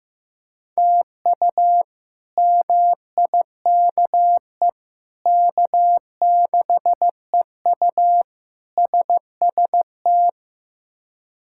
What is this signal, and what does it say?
Morse code, heard as a keyed tone, about 15 wpm, 700 Hz. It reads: TU MIKE K6EU SST
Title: K1USN SST QSO Practice - 15wpm